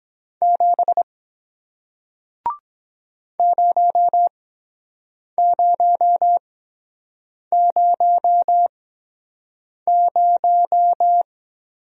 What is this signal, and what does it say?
Morse code, heard as a keyed tone, about 23 wpm, spaced as for 7 wpm, 700 Hz. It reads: 7 0000